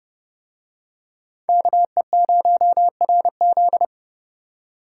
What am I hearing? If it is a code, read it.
Morse code, heard as a keyed tone, about 27 wpm, 700 Hz. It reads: KE0RZ